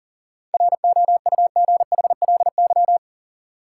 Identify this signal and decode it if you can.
Morse code, heard as a keyed tone, about 38 wpm, 700 Hz. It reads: ROUGHLY
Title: Sentences from Top 1500 Words - New Word Drill Alt - 40wpm